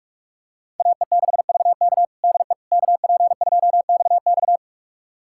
Morse code, heard as a keyed tone, about 46 wpm, 700 Hz. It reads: AE6VK DE KP2XX
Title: USA Callsign DE Callsign - 5 Characters - 45wpm